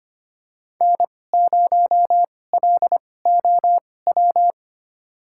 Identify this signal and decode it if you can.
Morse code, heard as a keyed tone, about 25 wpm, spaced as for 18 wpm, 700 Hz. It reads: N0LOW
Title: USA Call Signs - 5 Characters - 2x Character Spacing - 25wpm